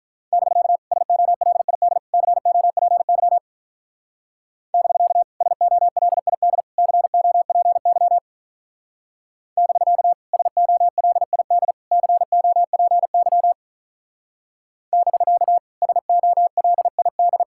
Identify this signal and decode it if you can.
Morse code, heard as a keyed tone, about 44 wpm, 700 Hz. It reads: <BK> SOLID COPY <BK> SOLID COPY <BK> SOLID COPY <BK> SOLID